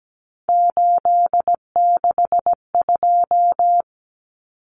Morse code, heard as a keyed tone, about 17 wpm, 700 Hz. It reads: 862
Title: Numbers - 3-Digit 17wpm